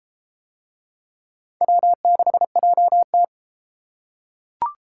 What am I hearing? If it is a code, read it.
Morse code, heard as a keyed tone, about 33 wpm, 700 Hz. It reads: W6JT